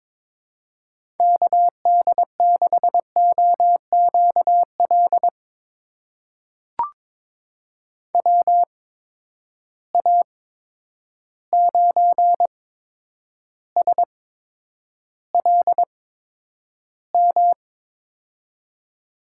Morse code, heard as a keyed tone, about 22 wpm, spaced as for 9 wpm, 700 Hz. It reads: KD6OQL WA9SLM
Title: USA Call Signs - 6 Characters - 8x Character Spacing - 22wpm